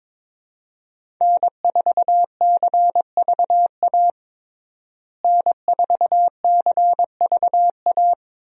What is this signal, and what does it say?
Morse code, heard as a keyed tone, about 22 wpm, 700 Hz. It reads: N4CVA N4CVA
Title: USA Call Signs - 5 Characters - Repeated 3x - 22wpm